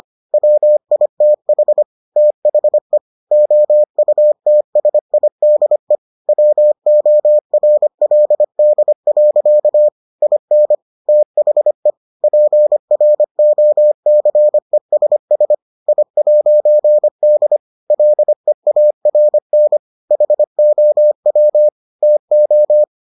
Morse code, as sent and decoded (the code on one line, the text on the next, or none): .-- .. - .... / - .... / --- ..- - ... .. -.. / .-- --- .-. .-.. -.. .-.-.- / .. -. / - .... / .--. .-. --- -.-. ... ... / .. .----. -.. / .-.. .- .-. -. / .... --- .-- / - ---
WITH THE OUTSIDE WORLD. IN THE PROCESS I'D LEARN HOW TO